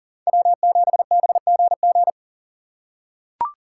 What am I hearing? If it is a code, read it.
Morse code, heard as a keyed tone, about 40 wpm, 700 Hz. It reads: W7BGG
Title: USA Call Signs - 5 Characters - 4x Character Spacing - 40wpm